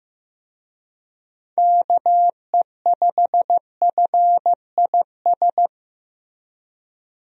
Morse code, heard as a keyed tone, about 15 wpm, 700 Hz. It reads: KE5FIS
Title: USA Call Signs - 6 Characters - Repeated 2x - 15wpm